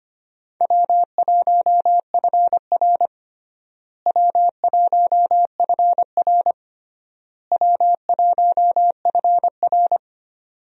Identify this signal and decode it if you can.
Morse code, heard as a keyed tone, about 25 wpm, 700 Hz. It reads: W1FR W1FR W1FR
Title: USA Call Signs - 4 Characters - Repeated 3x - 25wpm